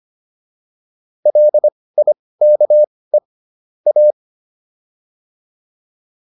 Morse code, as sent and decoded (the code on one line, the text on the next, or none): .-.. .. -.- . / .-
LIKE A